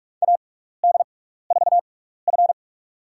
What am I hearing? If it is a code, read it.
Morse code, heard as a keyed tone, about 46 wpm, spaced as for 15 wpm, 700 Hz. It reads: AD4F
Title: USA Call Signs - 4 Characters - 6x Character Spacing - 45wpm